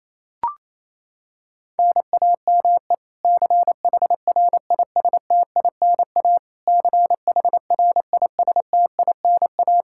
Morse code, heard as a keyed tone, about 27 wpm, 700 Hz. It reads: NAME CHRISTINA CHRISTINA
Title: K1USN SST QSO Practice 28wpm